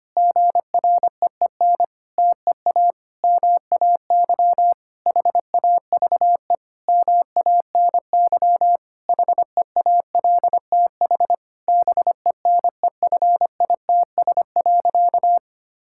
Morse code, heard as a keyed tone, about 25 wpm, 700 Hz. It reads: GREEN TEA MAY HAVE MANY HEALTH BENEFITS.
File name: practice / 25wpm / Difficult Sentences with Hints - 25wpm